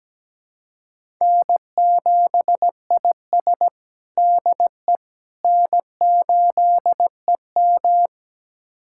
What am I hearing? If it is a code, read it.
Morse code, heard as a keyed tone, about 17 wpm, 700 Hz. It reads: N7IS DE N8EM
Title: USA Callsign DE Callsign - 4 Characters - 17wpm